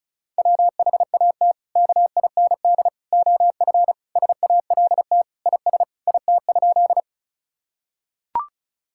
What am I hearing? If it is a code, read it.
Morse code, heard as a keyed tone, about 36 wpm, 700 Hz. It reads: WHAT KIND OF SALT IS IT?
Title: Sentences from Top 1000 Words - Review Part 5 - 35wpm